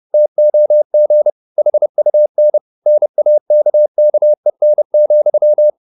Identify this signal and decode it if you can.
Morse code, heard as a keyed tone, about 30 wpm, 600 Hz. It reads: TOG HUN NAKKEN,
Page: Ferie 30 wpm